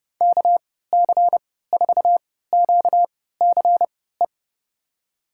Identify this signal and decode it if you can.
Morse code, heard as a keyed tone, about 30 wpm, spaced as for 17 wpm, 700 Hz. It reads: KC4QCE
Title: USA Call Signs - 6 Characters - 3x Character Spacing - 30wpm